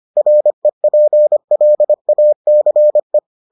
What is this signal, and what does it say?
Morse code, heard as a keyed tone, about 25 wpm, 600 Hz. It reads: REPLACE